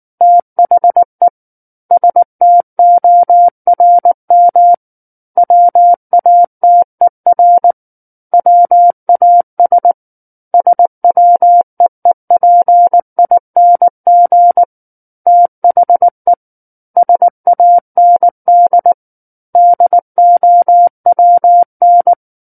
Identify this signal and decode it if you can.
Morse code, heard as a keyed tone, about 19 wpm, 700 Hz. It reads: THE STORM WATER WAS SWEEPING THE SAND DOWN
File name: Chapter 10 - 19 WPM